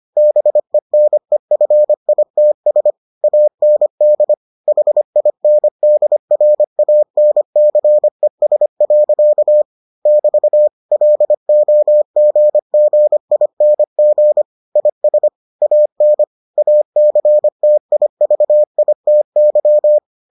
Morse code, heard as a keyed tone, about 25 wpm, 600 Hz. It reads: BENEFITS AND HINDRANCES. <BT> LOGGING IS AN ACTIVITY